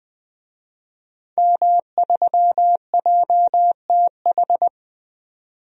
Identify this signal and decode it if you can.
Morse code, heard as a keyed tone, about 20 wpm, 700 Hz. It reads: M3JTH
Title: DXCC-100 Call Signs - 5 Characters - 20wpm